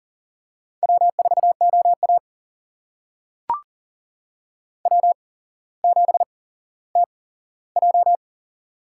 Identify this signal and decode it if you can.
Morse code, heard as a keyed tone, about 40 wpm, spaced as for 17 wpm, 700 Hz. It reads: W4OA W7TJ
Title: USA Call Signs - 4 Characters - 8x Character Spacing - 40wpm